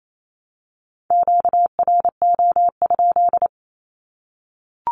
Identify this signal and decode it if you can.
Morse code, heard as a keyed tone, about 28 wpm, 700 Hz. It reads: QRO?
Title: Q-Codes 28wpm